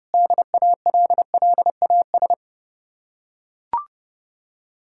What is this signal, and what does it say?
Morse code, heard as a keyed tone, about 29 wpm, 700 Hz. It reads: DALLAS